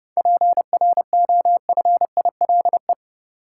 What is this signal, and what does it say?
Morse code, heard as a keyed tone, about 30 wpm, 700 Hz. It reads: PROFILE
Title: Seven Letter Words - 30wpm